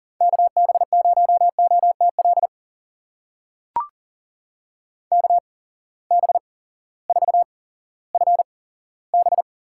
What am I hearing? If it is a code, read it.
Morse code, heard as a keyed tone, about 40 wpm, spaced as for 18 wpm, 700 Hz. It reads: KB0OTL KB4FB
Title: USA Call Signs - 8x Character Spacing - 40wpm